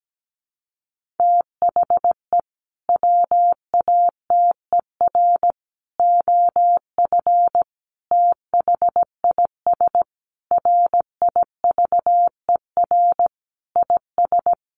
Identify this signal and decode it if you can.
Morse code, heard as a keyed tone, about 17 wpm, 700 Hz. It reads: THE WATER OF THIS RIVER IS
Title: Sentences from Top 1500 Words 17wpm